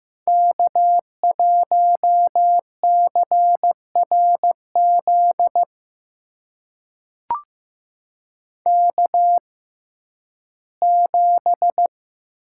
Morse code, heard as a keyed tone, about 15 wpm, spaced as for 11 wpm, 700 Hz. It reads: K1CRZ K7
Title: USA Call Signs - 6x Character Spacing - 15wpm